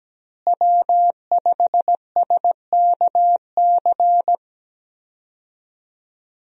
Morse code, heard as a keyed tone, about 17 wpm, 700 Hz. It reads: W5SKC
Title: USA Call Signs - 5 Characters - Repeated 2x - 17wpm